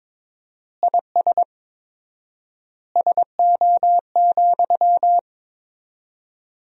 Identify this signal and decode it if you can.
Morse code, heard as a keyed tone, about 22 wpm, 700 Hz. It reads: IS SO,
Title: 4x Word-Spacing - Sentences from Top 200 Words - 22wpm